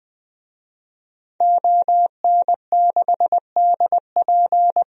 Morse code, heard as a keyed tone, about 20 wpm, 700 Hz. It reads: ON6DP
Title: DXCC-100 Call Signs - 5 Characters - 20wpm